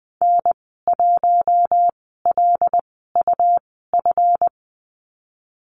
Morse code, heard as a keyed tone, about 20 wpm, 700 Hz. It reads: N 1 L U F
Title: 154 Lesson 25g - Letter p - Call Signs 20wpm